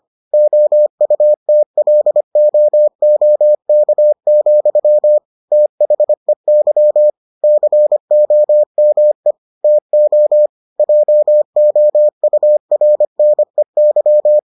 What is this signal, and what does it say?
Morse code, heard as a keyed tone, about 25 wpm, 600 Hz. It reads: OUTLOOK, THEY COME TO JOURNEY